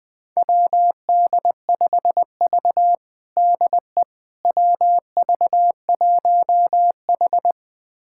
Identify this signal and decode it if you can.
Morse code, heard as a keyed tone, about 21 wpm, 700 Hz. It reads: WD5V DE WV1H
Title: USA Callsign DE Callsign - 4 Characters - 20wpm